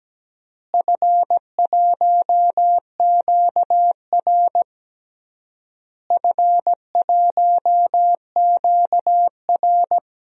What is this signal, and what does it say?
Morse code, heard as a keyed tone, about 17 wpm, 700 Hz. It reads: F1QR F1QR
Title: DXCC-100 Call Signs - 4 Characters - Repeated 3x - 17wpm